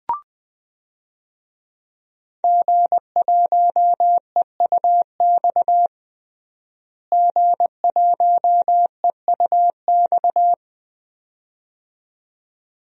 Morse code, heard as a keyed tone, about 20 wpm, 700 Hz. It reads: G1EUX G1EUX
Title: DXCC-100 Call Signs - 5 Characters - Repeated 2x - 20wpm